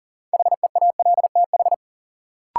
Morse code, heard as a keyed tone, about 40 wpm, 700 Hz. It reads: HEALTH
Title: Sets of 5 Words - New Word Drill - 40wpm